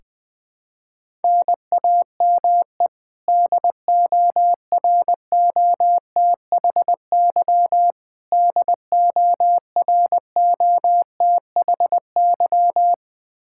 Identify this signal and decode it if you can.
Morse code, heard as a keyed tone, about 21 wpm, 700 Hz. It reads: NAME DOROTHY DOROTHY
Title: Practice QSO - Exchange 1 - Beginning - 20wpm